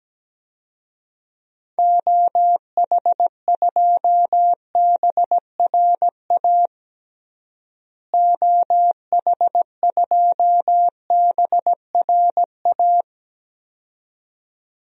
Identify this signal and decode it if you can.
Morse code, heard as a keyed tone, about 17 wpm, 700 Hz. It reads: OH2BRA OH2BRA
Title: DXCC-100 Call Signs - 6 Characters - Repeated 2x - 17wpm